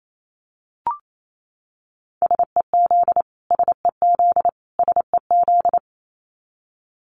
Morse code, heard as a keyed tone, about 28 wpm, 700 Hz. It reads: SEZ SEZ SEZ
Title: Abbreviations Repeated 3x - 28wpm